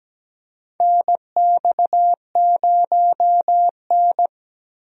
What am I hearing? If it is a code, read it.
Morse code, heard as a keyed tone, about 17 wpm, 700 Hz. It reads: NX0N
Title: USA Call Signs - 4 Characters - Repeated 2x - 17wpm